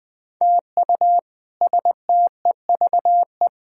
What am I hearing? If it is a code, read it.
Morse code, heard as a keyed tone, about 20 wpm, 700 Hz. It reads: TU STEVE
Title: K1USN SST QSO Practice - 20wpm